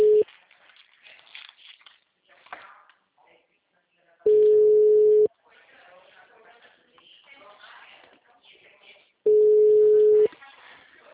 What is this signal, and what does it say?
Ringback tone (Europe, 425 Hz), with speech.
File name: out- -2006- - - .84702